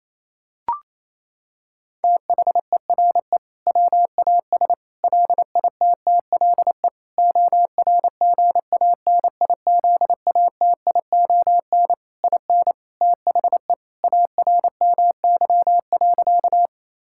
Morse code, heard as a keyed tone, about 28 wpm, 700 Hz. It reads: THERE WAS LITTLE ORGANIZATION IN THE ARMY.
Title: Sentences from Top 1000 Words - Review Part 5 - 28wpm